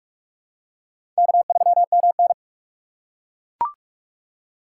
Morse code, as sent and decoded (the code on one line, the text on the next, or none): -.- ...-- -- -.
K3MN